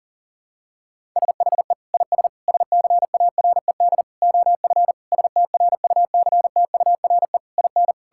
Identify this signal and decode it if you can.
Morse code, heard as a keyed tone, about 40 wpm, 700 Hz. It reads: SHE IS SCARED OF STRUCTURE IN
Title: Sentences from Top 1000 Words - Review Part 6 - 40wpm